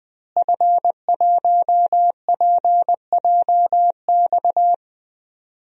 Morse code, heard as a keyed tone, about 21 wpm, 700 Hz. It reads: F1PJX
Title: DXCC-100 Call Signs - 5 Characters - Repeated 3x - 20wpm